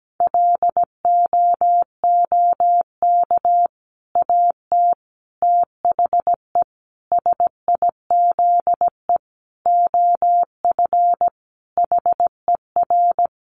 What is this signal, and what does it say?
Morse code, heard as a keyed tone, about 17 wpm, 700 Hz. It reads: LOOK AT THE SIZE OF HER
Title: Sentences from Top 1200 Words 17wpm